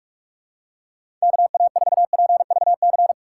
Morse code, heard as a keyed tone, about 41 wpm, 700 Hz. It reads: KA4PVC
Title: USA Call Signs - 6 Characters - 3x Character Spacing - 45wpm